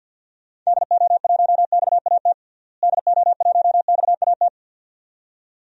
Morse code, heard as a keyed tone, about 47 wpm, 700 Hz. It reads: DO1XAT DO1XAT
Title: DXCC-100 Call Signs - 6 Characters - Repeated 2x - 50wpm